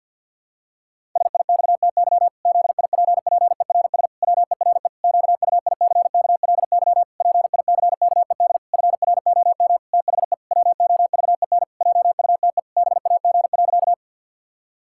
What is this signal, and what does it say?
Morse code, heard as a keyed tone, about 50 wpm, 700 Hz. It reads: SIXTY ZIPPERS WERE QUICKLY PICKED FROM THE WOVEN JUTE BAG.
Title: Pangram Challenge v2 - 50wpm